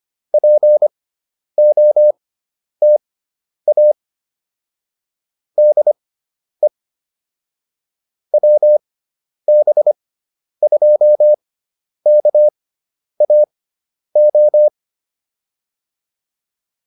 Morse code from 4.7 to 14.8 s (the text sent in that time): DE WB2KAO